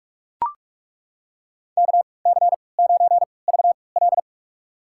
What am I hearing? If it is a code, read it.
Morse code, heard as a keyed tone, about 41 wpm, spaced as for 25 wpm, 700 Hz. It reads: KC9VL
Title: USA Call Signs - 5 Characters - 3x Character Spacing - 45wpm